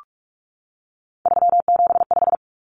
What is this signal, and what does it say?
Morse code, heard as a keyed tone, about 44 wpm, 700 Hz. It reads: 375
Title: Common Cut Numbers - 3-Digit 45wpm